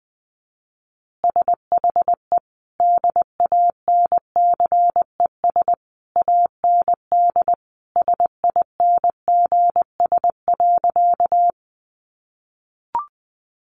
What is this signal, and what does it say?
Morse code, heard as a keyed tone, about 20 wpm, 700 Hz. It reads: SHE DANCES AND SINGS.
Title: Easy Phrases 20wpm